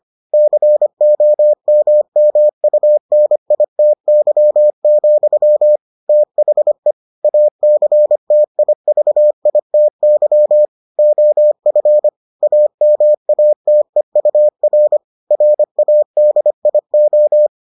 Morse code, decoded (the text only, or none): COMMUNITY, THE ACTIVITY OF AMATEUR RADIO